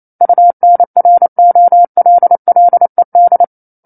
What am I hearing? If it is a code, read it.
Morse code, heard as a keyed tone, about 28 wpm, 700 Hz. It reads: UNROLLED
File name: 09-Chapter-09 28 WPM